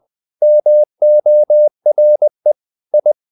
Morse code, heard as a keyed tone, about 20 wpm, 600 Hz. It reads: MORE I
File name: the old man and the sea part 2 20wpm